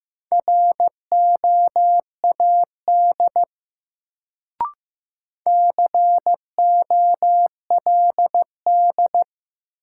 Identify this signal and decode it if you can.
Morse code, heard as a keyed tone, about 15 wpm, 700 Hz. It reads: ROAD COLD
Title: Sentences from Top 400 Words - New Word Drill - 15wpm